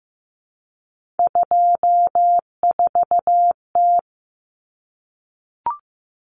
Morse code, heard as a keyed tone, about 15 wpm, 700 Hz. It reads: 24T
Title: Common Cut Numbers - 3-Digit 15wpm